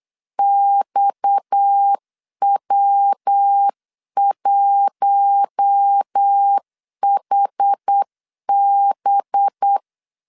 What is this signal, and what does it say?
Morse code, heard as a keyed tone, about 8 wpm, 800 Hz. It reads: XW1HB
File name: CW / DET / 1 6 1500C S C 8wpm 5219